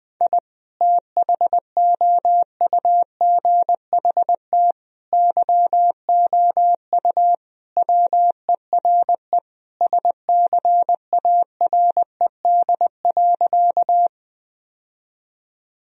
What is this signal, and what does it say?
Morse code, heard as a keyed tone, about 20 wpm, 700 Hz. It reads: I THOUGHT YOU WERE SCARED.
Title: Sentences from Top 1000 Words - Review Part 4 - 20wpm